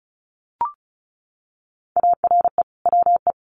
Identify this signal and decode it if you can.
Morse code, heard as a keyed tone, about 34 wpm, 700 Hz. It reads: ARE WE
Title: Sentences from Top 1300 Words 35wpm